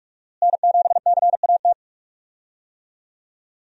Morse code, heard as a keyed tone, about 46 wpm, 700 Hz. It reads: N7CAT
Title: USA Call Signs - 5 Characters - Repeated 3x - 45wpm